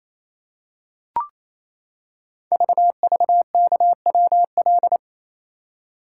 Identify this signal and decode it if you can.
Morse code, heard as a keyed tone, about 27 wpm, 700 Hz. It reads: VVKWL
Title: Code Groups - 5 Letters - 28wpm